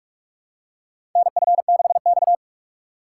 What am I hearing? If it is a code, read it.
Morse code, heard as a keyed tone, about 46 wpm, 700 Hz. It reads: NF6X
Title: USA Call Signs - 4 Characters - 6x Character Spacing - 45wpm